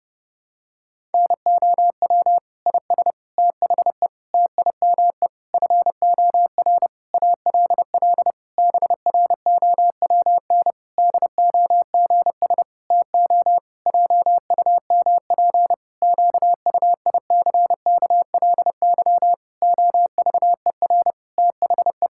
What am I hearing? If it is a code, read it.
Morse code, heard as a keyed tone, about 30 wpm, 700 Hz. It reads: NOW IS THE TIME FOR ALL BROWN DOGS TO JUMP QUICKLY OVER THE
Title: Pangram Challenge v2 - 30wpm